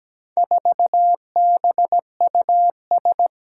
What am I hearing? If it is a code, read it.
Morse code, heard as a keyed tone, about 17 wpm, 700 Hz. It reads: 4BUS
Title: Four Challenging Letter-Number Combinations 17wpm